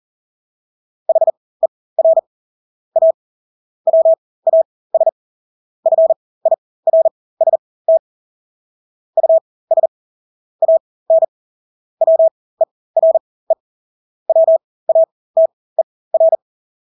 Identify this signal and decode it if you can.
Morse code, heard as a keyed tone, about 40 wpm, spaced as for 20 wpm, 650 Hz. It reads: HER A WAS FIRST US AN WERE WATER
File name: Engelske ord 40-20 wpm